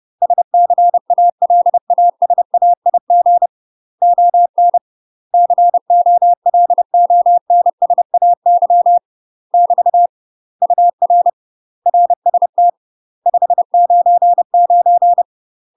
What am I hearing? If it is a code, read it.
Morse code, heard as a keyed tone, about 30 wpm, 700 Hz. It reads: SCALASAIG ON COLONSAY <BT> UR RST 599
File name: UK MASTER QSOs TRAINING / 30WPM / UK MASTER QSOs 0000